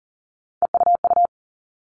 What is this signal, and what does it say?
Morse code, heard as a keyed tone, about 40 wpm, 700 Hz. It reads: EUU
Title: All Cut Numbers - 3-Digit 40wpm